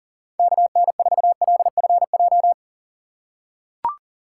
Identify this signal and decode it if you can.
Morse code, heard as a keyed tone, about 40 wpm, 700 Hz. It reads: KN4LFJ
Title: USA Call Signs - 6 Characters - 3x Character Spacing - 40wpm